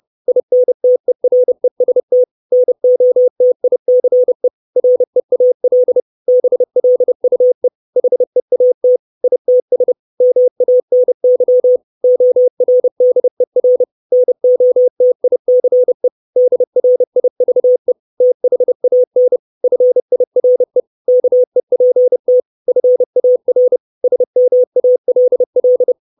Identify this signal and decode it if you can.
Morse code, heard as a keyed tone, about 30 wpm, 500 Hz. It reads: INTEREST NOTICE REAL BLUE HEAT ITS MANY ORDER NOTICE DRIVE THAN FIRE KEPT FAR SMALL
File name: 632 Words 30WPM